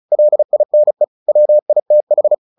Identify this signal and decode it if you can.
Morse code, heard as a keyed tone, about 34 wpm, 600 Hz. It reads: LINE WITH
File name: the old man and the sea 2 35wpm